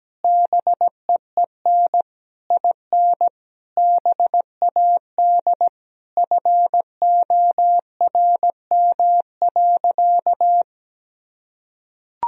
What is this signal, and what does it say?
Morse code, heard as a keyed tone, about 17 wpm, 700 Hz. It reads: BEEN IN BAD FORM.
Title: Sentences from Top 1000 Words - Review Part 1 - 17wpm